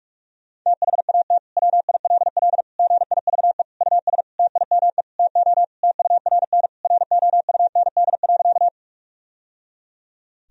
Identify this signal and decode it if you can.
Morse code, heard as a keyed tone, about 46 wpm, 700 Hz. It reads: THAT WILL GIVE US TIME TO TURN ROUND.
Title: Sentences from Top 1000 Words - Review Part 1 - 45wpm